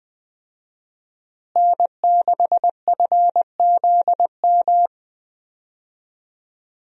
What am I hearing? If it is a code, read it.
Morse code, heard as a keyed tone, about 21 wpm, 700 Hz. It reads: N6FZM
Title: USA Call Signs - 5 Characters - Repeated 3x - 20wpm